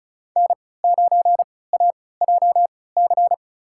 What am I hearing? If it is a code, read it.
Morse code, heard as a keyed tone, about 36 wpm, spaced as for 20 wpm, 700 Hz. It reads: N9AJC
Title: USA Call Signs - 3x Character Spacing - 35wpm